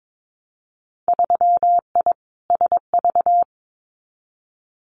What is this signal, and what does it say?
Morse code, heard as a keyed tone, about 22 wpm, 700 Hz. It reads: 3I SV
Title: Winter Field Day Exchanges Encore 22wpm